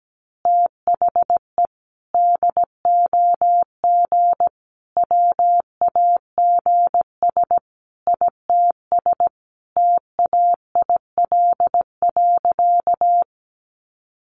Morse code, heard as a keyed tone, about 17 wpm, 700 Hz. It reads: THE DOG WAGS ITS TAIL.
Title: Easy Phrases 17wpm